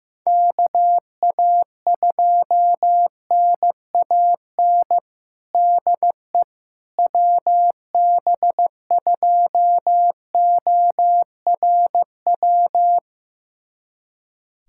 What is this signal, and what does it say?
Morse code, heard as a keyed tone, about 15 wpm, 700 Hz. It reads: KA2NAN DE WB2ORW
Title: USA Callsign DE Callsign 15wpm